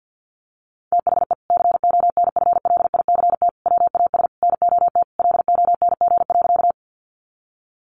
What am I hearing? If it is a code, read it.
Morse code, heard as a keyed tone, about 50 wpm, 700 Hz. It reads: THE CONFLICT WAS NOT LONG.
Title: Sentences from Top 1300 Words 50wpm